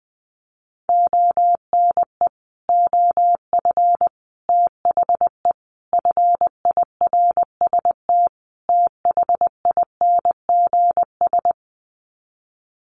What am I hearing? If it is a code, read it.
Morse code, heard as a keyed tone, about 20 wpm, 700 Hz. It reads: ONE OF THE FIRST THINGS
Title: Sets of 5 Words Encore 20wpm